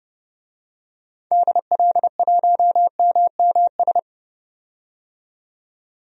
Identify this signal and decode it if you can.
Morse code, heard as a keyed tone, about 30 wpm, 700 Hz. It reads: DL1MMS
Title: DXCC-100 Call Signs - 6 Characters - 30wpm